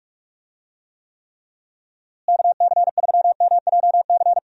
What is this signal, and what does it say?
Morse code, heard as a keyed tone, about 46 wpm, 700 Hz. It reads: KC3MJC
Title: USA Call Signs - 6 Characters - Repeated 3x - 45wpm